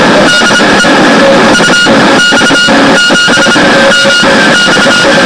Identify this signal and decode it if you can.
Morse code, heard as a keyed tone, about 26 wpm, spaced as for 18 wpm, 1430 Hz, with noise. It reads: DE UX7MX